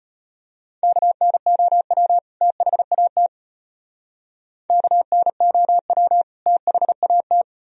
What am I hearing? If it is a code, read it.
Morse code, heard as a keyed tone, about 33 wpm, 700 Hz. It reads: KNOW THAT KNOW THAT